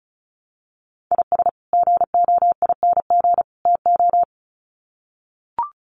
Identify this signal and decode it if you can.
Morse code, heard as a keyed tone, about 34 wpm, 700 Hz. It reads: IS GOING TO
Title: Sets of 3 Words 35wpm